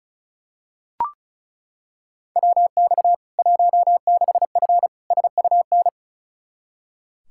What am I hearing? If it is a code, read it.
Morse code, heard as a keyed tone, about 35 wpm, 700 Hz. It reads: WX 16F SUN
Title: K1USN SST QSO Practice 35wpm